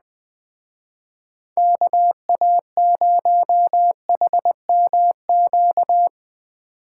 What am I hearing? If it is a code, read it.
Morse code, heard as a keyed tone, about 21 wpm, 700 Hz. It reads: KA0HMQ